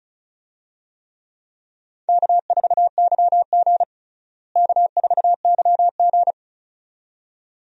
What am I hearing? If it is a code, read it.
Morse code, heard as a keyed tone, about 34 wpm, 700 Hz. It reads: K4YG K4YG
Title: USA Call Signs - 4 Characters - Repeated 2x - 35wpm